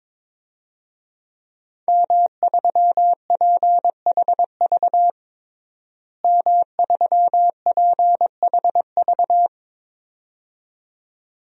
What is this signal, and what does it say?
Morse code, heard as a keyed tone, about 22 wpm, 700 Hz. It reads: M3PHV M3PHV
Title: DXCC-100 Call Signs - 5 Characters - Repeated 2x - 22wpm